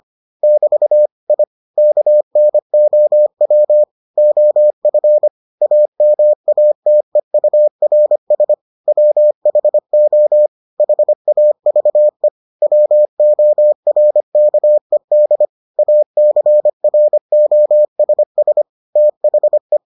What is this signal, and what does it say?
Morse code, heard as a keyed tone, about 25 wpm, 600 Hz. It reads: <BT> I KNOW OF AMATEURS WHO HAVE WORKED ACROSS THE